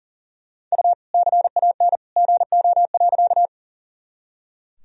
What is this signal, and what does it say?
Morse code, heard as a keyed tone, about 40 wpm, 700 Hz. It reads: U CAN GO.
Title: Sentences from Top 100 Words and Abbreviations 40wpm